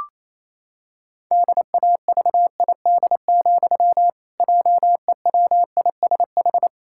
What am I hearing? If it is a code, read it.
Morse code, heard as a keyed tone, about 28 wpm, 700 Hz. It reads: DAVID, JEWISH
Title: Pangram Challenge v2 - 28wpm